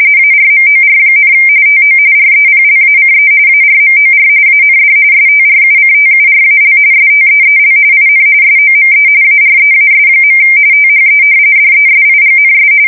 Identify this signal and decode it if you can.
RTTY, about 45.45 baud. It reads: ARMOR, TO REPLACE ARMOR AND STRUCTURAL STEEL IN SOME AREAS. THE DESIGNS
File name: fsk170 / battleshipnj